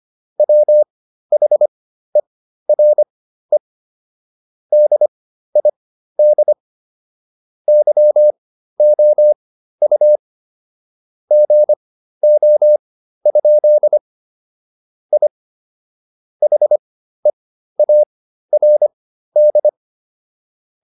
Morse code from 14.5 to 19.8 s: I HEARD

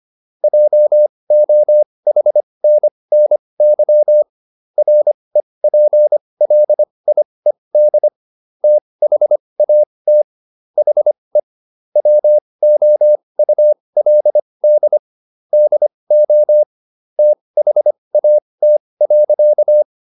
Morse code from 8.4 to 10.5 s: THAT

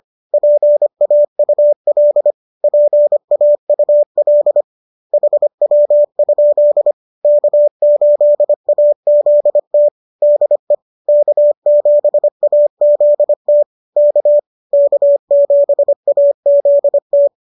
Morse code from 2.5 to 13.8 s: PAUL HW? K8AZT DE K7AZT

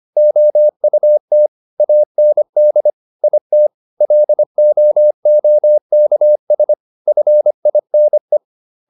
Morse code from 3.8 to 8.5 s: LOOKS FINE

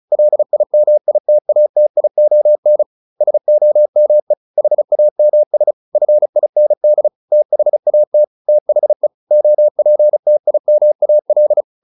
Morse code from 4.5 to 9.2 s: HAMS FIND THAT THE

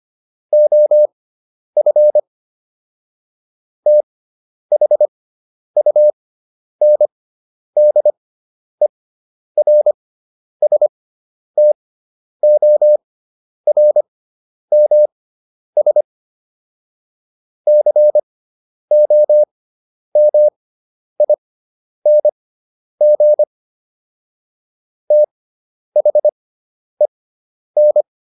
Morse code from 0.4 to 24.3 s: OF THUNDERSTORMS COMING

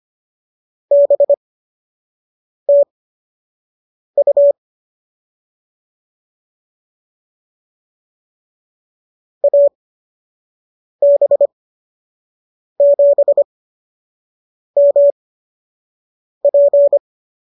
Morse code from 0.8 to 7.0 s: BTU

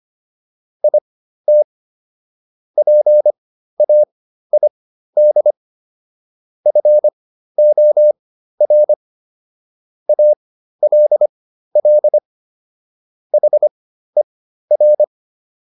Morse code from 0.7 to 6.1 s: IT PAID

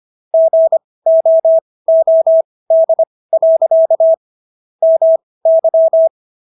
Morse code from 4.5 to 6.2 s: MY

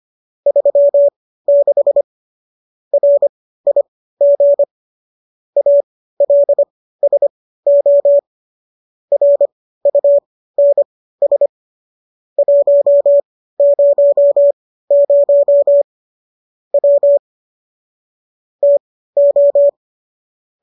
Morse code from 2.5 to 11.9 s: RIG ALSO RUNS

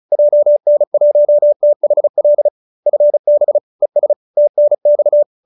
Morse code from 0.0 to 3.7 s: JN1THL FB

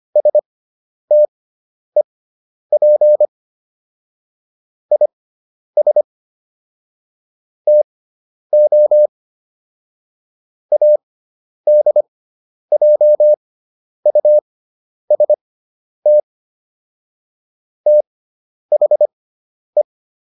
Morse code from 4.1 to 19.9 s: IS TO ADJUST THE

